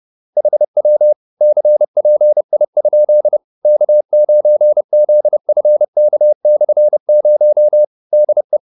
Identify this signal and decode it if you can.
Morse code, heard as a keyed tone, about 30 wpm, 600 Hz. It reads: HW CPI? K9ZFK/0 DE